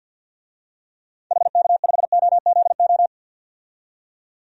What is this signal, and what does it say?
Morse code, heard as a keyed tone, about 52 wpm, 700 Hz. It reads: HK5OZO
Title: DXCC-100 Call Signs - 6 Characters - 50wpm